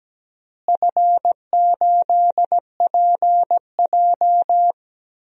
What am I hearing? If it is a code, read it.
Morse code, heard as a keyed tone, about 17 wpm, 700 Hz. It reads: F8PJ
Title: DXCC-100 Call Signs - 4 Characters - Repeated 2x - 17wpm